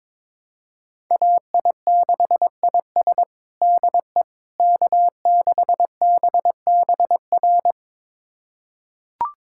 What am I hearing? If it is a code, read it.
Morse code, heard as a keyed tone, about 22 wpm, 700 Hz. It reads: AI6IS DE K6BBR
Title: USA Callsign DE Callsign - 5 Characters - 22wpm